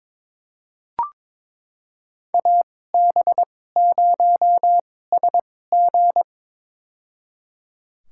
Morse code, heard as a keyed tone, about 23 wpm, spaced as for 16 wpm, 700 Hz. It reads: AB0SG
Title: USA Call Signs - 2x Character Spacing - 22wpm